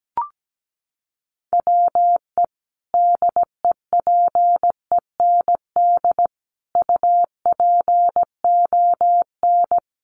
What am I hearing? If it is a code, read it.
Morse code, heard as a keyed tone, about 17 wpm, 700 Hz. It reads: WE DEPEND UPON
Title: Sentences from Top 1300 Words 17wpm